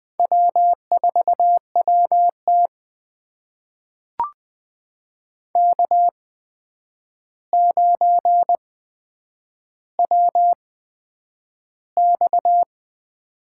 Morse code, heard as a keyed tone, about 20 wpm, spaced as for 9 wpm, 700 Hz. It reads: W4WT K9WX
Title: USA Call Signs - 4 Characters - 8x Character Spacing - 20wpm